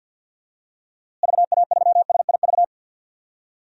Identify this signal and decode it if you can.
Morse code, heard as a keyed tone, about 52 wpm, 700 Hz. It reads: VA3SIV
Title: DXCC-100 Call Signs - 6 Characters - Repeated 2x - 50wpm